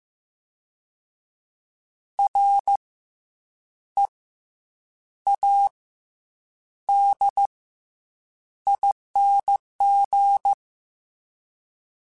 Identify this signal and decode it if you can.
Morse code, heard as a keyed tone, about 15 wpm, 800 Hz. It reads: R E A D ING
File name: Special INGs / Special INGs-mp3s / SlowToFaster ing @13